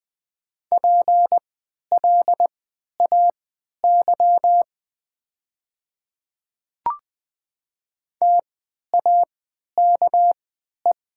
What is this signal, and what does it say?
Morse code, heard as a keyed tone, about 20 wpm, spaced as for 11 wpm, 700 Hz. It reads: PLAY TAKE
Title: 3x Character Spacing - Top 300 Words - 20wpm